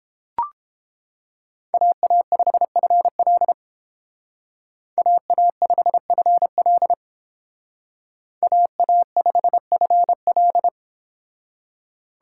Morse code, heard as a keyed tone, about 31 wpm, 700 Hz. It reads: AA5FL AA5FL AA5FL